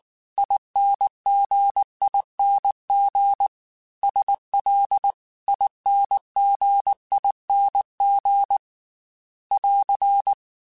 Morse code, heard as a keyed tone, about 19 wpm, 800 Hz. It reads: INGING SLINGING <AR>
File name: Special INGs-mp3s / Special INGs-mp3s / Dble Ings sp @18